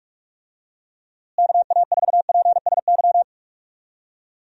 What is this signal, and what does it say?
Morse code, heard as a keyed tone, about 42 wpm, 700 Hz. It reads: KA4PSY
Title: USA Call Signs - 6 Characters - Repeated 2x - 45wpm